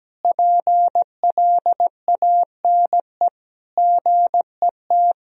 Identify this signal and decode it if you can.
Morse code, heard as a keyed tone, about 17 wpm, 700 Hz. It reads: PLANE GET
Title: Sentences from Top 1000 Words - Review Part 3 - 17wpm